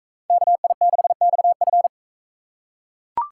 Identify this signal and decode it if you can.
Morse code, heard as a keyed tone, about 40 wpm, 700 Hz. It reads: KI6XF